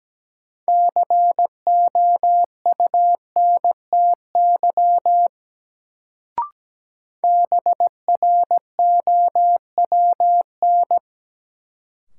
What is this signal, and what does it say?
Morse code, heard as a keyed tone, about 17 wpm, 700 Hz. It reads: COUNTY BROWN
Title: Sentences from Top 1200 Words - New Word Drill Alt - 17wpm